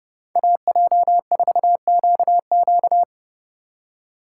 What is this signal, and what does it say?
Morse code, heard as a keyed tone, about 39 wpm, 700 Hz. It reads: AJ4QQ